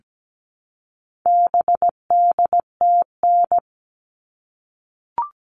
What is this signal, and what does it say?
Morse code, heard as a keyed tone, about 17 wpm, 700 Hz. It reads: BDTN
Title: All Cut Numbers - 4-Digit 17wpm